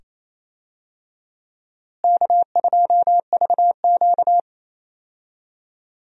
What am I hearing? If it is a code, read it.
Morse code, heard as a keyed tone, about 28 wpm, 700 Hz. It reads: K2VQ
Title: USA Call Signs - 4 Characters - Repeated 2x - 28wpm